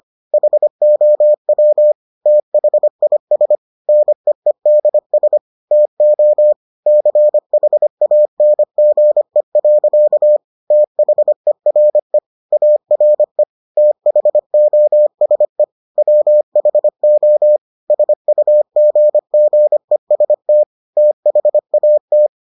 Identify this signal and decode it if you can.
Morse code, heard as a keyed tone, about 25 wpm, 600 Hz. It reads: HOW THIS NEEDS TO CHANGE. THERE ARE THOSE WHO SUGGEST THAT